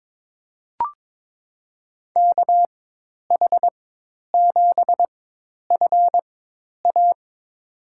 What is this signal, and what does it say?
Morse code, heard as a keyed tone, about 22 wpm, spaced as for 10 wpm, 700 Hz. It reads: KH7FA
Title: USA Call Signs - 5 Characters - 4x Character Spacing - 22wpm